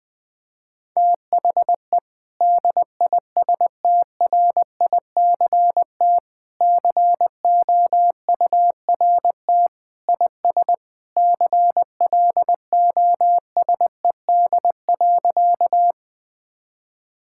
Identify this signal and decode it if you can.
Morse code, heard as a keyed tone, about 20 wpm, 700 Hz. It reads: THE DISTRICT COURT IS CLOSED.
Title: Sentences from Top 1600 Words 20wpm